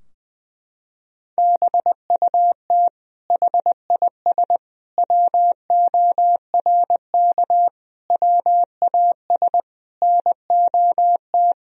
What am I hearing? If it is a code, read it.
Morse code, heard as a keyed tone, about 21 wpm, 700 Hz. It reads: BUT HIS WORK WAS NOT
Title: Sentences from Top 1000 Words - Review Part 2 - 20wpm